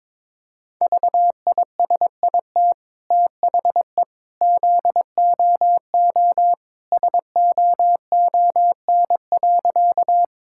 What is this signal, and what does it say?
Morse code, heard as a keyed tone, about 22 wpm, 700 Hz. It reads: VISIT THE ZOO SOON.
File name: practice / 22wpm / Easy Sentences with Hints - 22wpm